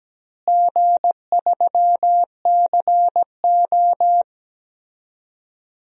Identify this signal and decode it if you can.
Morse code, heard as a keyed tone, about 17 wpm, 700 Hz. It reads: G3CO
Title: DXCC-100 Call Signs - 4 Characters - Repeated 2x - 17wpm